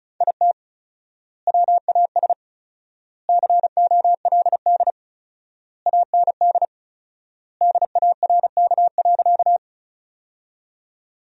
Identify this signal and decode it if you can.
Morse code, heard as a keyed tone, about 36 wpm, 700 Hz. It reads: IT WAS COLD AND DARK.
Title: 4x Word-Spacing - Sentences from Top 500 Words - 35wpm